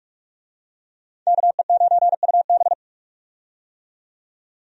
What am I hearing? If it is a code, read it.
Morse code, heard as a keyed tone, about 42 wpm, 700 Hz. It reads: KE9UB
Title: USA Call Signs - 5 Characters - 45wpm